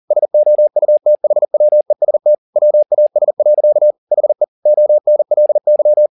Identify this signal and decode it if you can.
Morse code, heard as a keyed tone, about 40 wpm, 600 Hz. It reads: SOUTHWEST WAS. HE ONLY
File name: the old man and the sea 2 40wpm